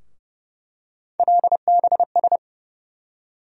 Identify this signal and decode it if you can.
Morse code, heard as a keyed tone, about 30 wpm, 700 Hz. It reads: LBS
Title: USA Call Sign Postfixes - 3 Letters 30wpm